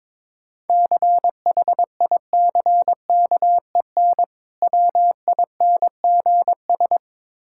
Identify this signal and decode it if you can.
Morse code, heard as a keyed tone, about 22 wpm, 690 Hz. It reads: CHICKEN WINGS
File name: Difficult Words with Hints - 22wpm